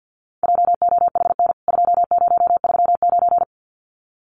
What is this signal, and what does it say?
Morse code, heard as a keyed tone, about 50 wpm, 700 Hz. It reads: JOHN 2039
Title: CWT Exchange 50wpm